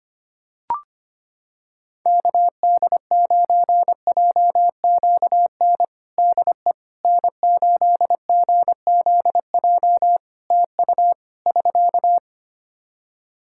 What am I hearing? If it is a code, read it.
Morse code, heard as a keyed tone, about 25 wpm, 700 Hz. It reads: KD9JQN DE N8GZJ TU <SK>
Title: K1USN SST QSO Practice 25wpm